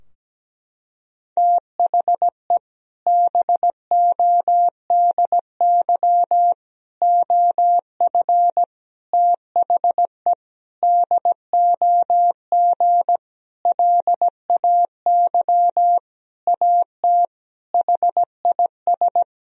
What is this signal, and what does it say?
Morse code, heard as a keyed tone, about 17 wpm, 700 Hz. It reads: THE BODY OF THE DOG LAY AT HIS
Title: Sentences from Top 1000 Words - Review Part 2 - 17wpm